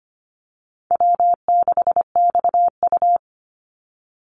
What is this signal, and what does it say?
Morse code, heard as a keyed tone, about 25 wpm, 700 Hz. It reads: W6XU
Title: Easy Call Signs - 25wpm